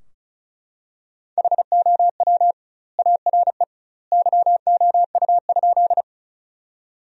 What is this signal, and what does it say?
Morse code, heard as a keyed tone, about 35 wpm, 700 Hz. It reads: HOW ARE YOU?
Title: 2x Word-Spacing - Sentences from Top 100 Words - 35wpm